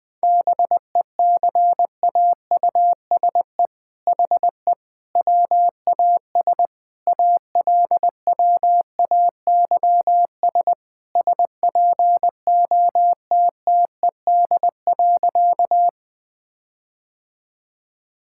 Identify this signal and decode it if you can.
Morse code, heard as a keyed tone, about 21 wpm, 700 Hz. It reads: BECAUSE HE WAS ALWAYS SPOTTED.
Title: All-Age Appropriate Jokes - 20wpm